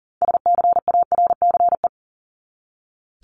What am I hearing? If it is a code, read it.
Morse code, heard as a keyed tone, about 40 wpm, 700 Hz. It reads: SCARCE
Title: LY Suffix - Top 100 Words with Base Words 40wpm